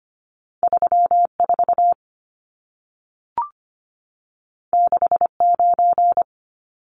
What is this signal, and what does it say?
Morse code, heard as a keyed tone, about 25 wpm, 700 Hz. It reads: 34 69
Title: Numbers - 2-Digit 25wpm